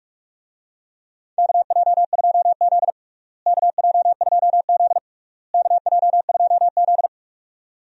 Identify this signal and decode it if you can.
Morse code, heard as a keyed tone, about 47 wpm, 700 Hz. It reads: KJ2Z KJ2Z KJ2Z
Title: USA Call Signs - 4 Characters - Repeated 3x - 45wpm